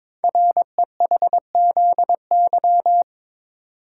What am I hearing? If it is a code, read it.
Morse code, heard as a keyed tone, about 23 wpm, 700 Hz. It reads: REHZY
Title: Code Groups - 5 Letters - 22wpm